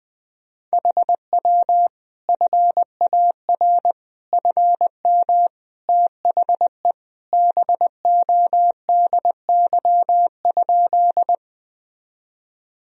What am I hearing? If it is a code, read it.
Morse code, heard as a keyed tone, about 19 wpm, 700 Hz. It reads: HW FAR FM THE BODY?
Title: Sentences from Top 400 Words and Abbreviations 20wpm